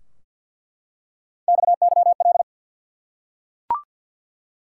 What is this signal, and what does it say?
Morse code, heard as a keyed tone, about 47 wpm, 700 Hz. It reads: XYL
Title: Top 180 QSO Elements 50wpm